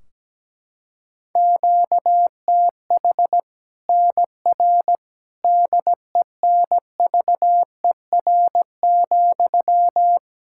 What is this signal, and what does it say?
Morse code, heard as a keyed tone, about 17 wpm, 700 Hz. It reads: QTH NR DENVER,
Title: K1USN SST QSO Practice 17wpm